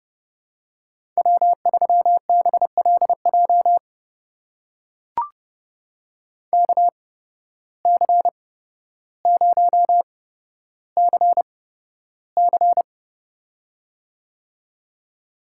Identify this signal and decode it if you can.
Morse code, heard as a keyed tone, about 32 wpm, spaced as for 13 wpm, 700 Hz. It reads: W3BLJ KC0CC
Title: USA Call Signs - 5 Characters - 8x Character Spacing - 30wpm